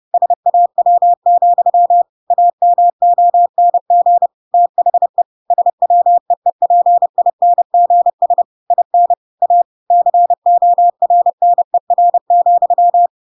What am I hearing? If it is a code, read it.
Morse code, heard as a keyed tone, about 30 wpm, 700 Hz. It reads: SAW, AMONG THE SWEEPINGS IN A CORNER,
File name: Pinocchio / Pinocchio ch5 30 30 ik1hge